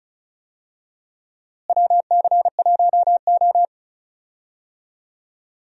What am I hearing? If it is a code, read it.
Morse code, heard as a keyed tone, about 36 wpm, 700 Hz. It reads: WC1O
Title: USA Call Signs - 4 Characters - Repeated 2x - 35wpm